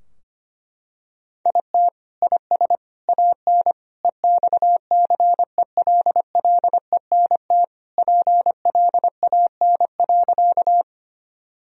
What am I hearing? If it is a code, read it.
Morse code, heard as a keyed tone, about 25 wpm, 700 Hz. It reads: IT IS AN EXCELLENT PLAN.
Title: Sentences from Top 1000 Words - Review Part 2 - 25wpm